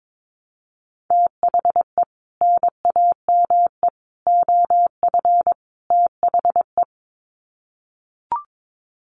Morse code, heard as a keyed tone, about 22 wpm, 700 Hz. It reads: THE NAME OF THE
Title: Sets of 4 Words 22wpm